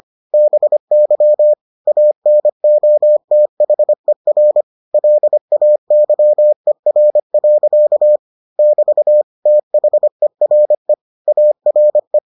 Morse code, heard as a keyed tone, about 25 wpm, 600 Hz. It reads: BY ANOTHER LAYER. <BT> THERE ARE